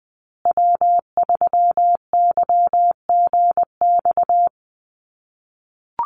Morse code, heard as a keyed tone, about 20 wpm, 700 Hz. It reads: W3YGX
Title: USA Call Signs Repeated 3x 20wpm